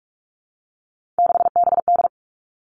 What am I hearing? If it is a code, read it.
Morse code, heard as a keyed tone, about 46 wpm, 700 Hz. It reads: 6BD
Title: Three Challenging Letter-Number Combinations 45wpm